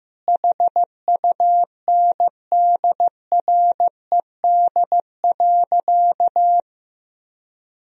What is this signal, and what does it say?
Morse code, heard as a keyed tone, about 15 wpm, 700 Hz. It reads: HUNDRED.
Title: Sentences from Top 1000 Words - Review Part 2 - 15wpm